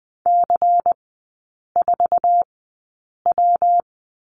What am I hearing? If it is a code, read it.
Morse code, heard as a keyed tone, about 20 wpm, spaced as for 8 wpm, 700 Hz. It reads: C4W
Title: 121 Lesson 20e - Number 3 - Three Characters 20wpm